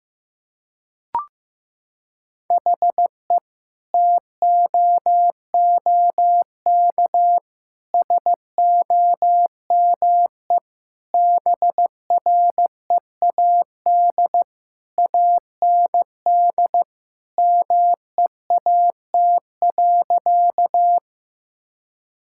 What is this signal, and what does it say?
Morse code, heard as a keyed tone, about 15 wpm, 700 Hz. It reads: HE TOOK SOME BREAD AND MEAT.
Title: Sentences from Top 1600 Words 15wpm